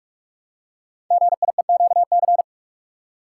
Morse code, heard as a keyed tone, about 44 wpm, 700 Hz. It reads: GIEQC
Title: Code Groups - 5 Letter:Numbers - 45wpm